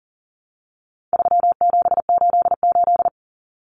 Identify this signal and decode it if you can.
Morse code, heard as a keyed tone, about 40 wpm, 700 Hz. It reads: 3788
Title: Common Cut Numbers - 4-Digit 40wpm